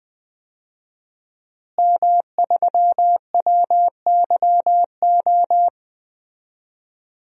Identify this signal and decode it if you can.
Morse code, heard as a keyed tone, about 20 wpm, 700 Hz. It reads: M3WYO
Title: DXCC-100 Call Signs - 5 Characters - Repeated 3x - 20wpm